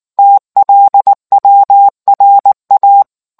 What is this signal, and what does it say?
Morse code, heard as a keyed tone, about 19 wpm, 800 Hz. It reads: TLWRA